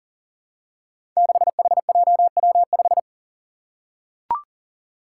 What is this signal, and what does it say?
Morse code, heard as a keyed tone, about 40 wpm, 700 Hz. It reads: 6HJW5
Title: Five Challenging Letter-Number Combinations 40wpm